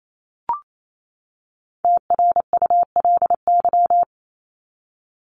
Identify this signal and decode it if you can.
Morse code, heard as a keyed tone, about 28 wpm, 700 Hz. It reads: TRULY
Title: Top 1501-1750 Words 28wpm